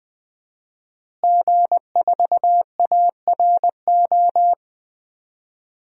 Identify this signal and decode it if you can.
Morse code, heard as a keyed tone, about 20 wpm, 700 Hz. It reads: G4ARO
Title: DXCC-100 Call Signs - 5 Characters - Repeated 2x - 20wpm